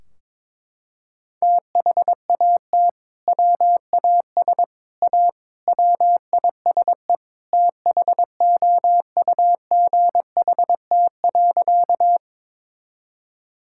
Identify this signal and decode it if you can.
Morse code, heard as a keyed tone, about 22 wpm, 700 Hz. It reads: THAT WAS A WISE THOUGHT.
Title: Sentences from Top 1600 Words 22wpm